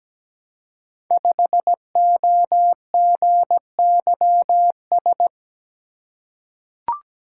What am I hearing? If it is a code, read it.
Morse code, heard as a keyed tone, about 17 wpm, 700 Hz. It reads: 5OGYS
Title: Code Groups - 5 Letter:Numbers - 17wpm